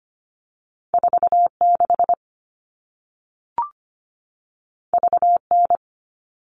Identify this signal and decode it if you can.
Morse code, heard as a keyed tone, about 25 wpm, 700 Hz. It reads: 4 6 V N
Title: All Cut Numbers - 2-Digit 25wpm